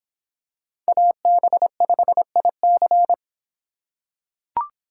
Morse code, heard as a keyed tone, about 24 wpm, 700 Hz. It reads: AB5IC